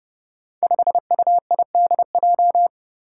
Morse code, heard as a keyed tone, about 30 wpm, 700 Hz. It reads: 5UIDJ
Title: Five Challenging Letter-Number Combinations 30wpm